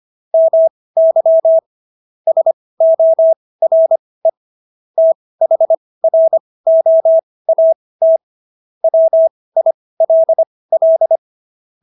Morse code, heard as a keyed tone, about 25 wpm, spaced as for 18 wpm, 650 Hz. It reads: MY SORE THROAT WILL